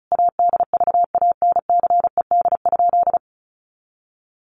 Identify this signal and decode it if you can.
Morse code, heard as a keyed tone, about 34 wpm, 700 Hz. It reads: ADVANCED?
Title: Sentences from Top 1200 Words 35wpm